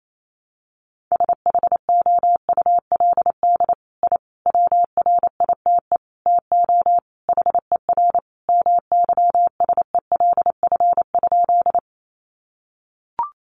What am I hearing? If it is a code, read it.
Morse code, heard as a keyed tone, about 28 wpm, 700 Hz. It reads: SHOULD I WRITE TO HER MYSELF?
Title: Sentences from Top 1300 Words 28wpm